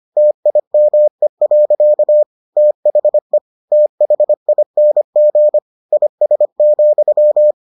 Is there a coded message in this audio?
Yes: Morse code from 0.2 to 7.5 s, heard as a keyed tone, about 25 wpm, 600 Hz. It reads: TIME. THE THING IS,